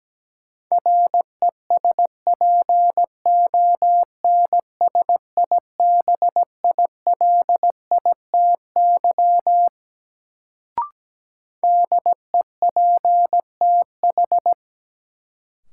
Morse code, heard as a keyed tone, about 17 wpm, 700 Hz. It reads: RESPONSIBILITY DEPTH
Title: Sentences from Top 1200 Words - New Word Drill - 17wpm